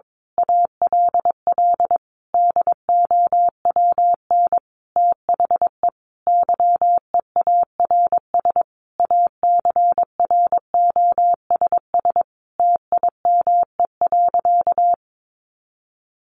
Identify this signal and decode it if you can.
Morse code, heard as a keyed tone, about 22 wpm, 700 Hz. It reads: ALL DOWN THE YEARS ACROSS TIME.
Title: A Sound Of Thunder 22wpm